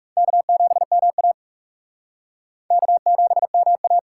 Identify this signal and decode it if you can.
Morse code, heard as a keyed tone, about 35 wpm, 700 Hz. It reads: K7MA K7MA
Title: Speed Racing - USA Call Signs 30wpm